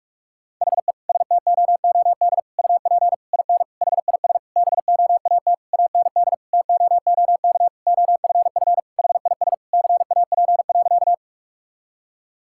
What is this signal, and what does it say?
Morse code, heard as a keyed tone, about 46 wpm, 700 Hz. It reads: HE STOOD UP IN HIS BOAT AND TOOK OFF HIS CAP.
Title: Sentences from Top 1600 Words 45wpm